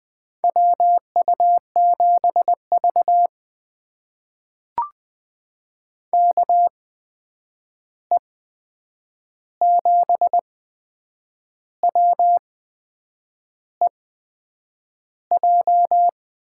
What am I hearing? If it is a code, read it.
Morse code, heard as a keyed tone, about 20 wpm, spaced as for 8 wpm, 700 Hz. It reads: WU7V KE7WEJ
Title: USA Call Signs - 8x Character Spacing - 20wpm